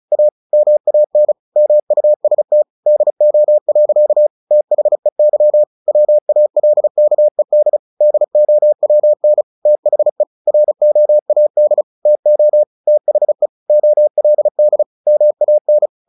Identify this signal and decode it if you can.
Morse code, heard as a keyed tone, about 35 wpm, 600 Hz. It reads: A MAN MUST DO. THEY WALKED DOWN THE ROAD TO THE OLD MAN
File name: the old man and the sea / the old man and the sea 1 35wpm